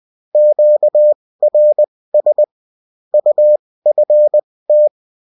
Morse code, heard as a keyed tone, about 20 wpm, spaced as for 16 wpm, 600 Hz. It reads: QRS UFT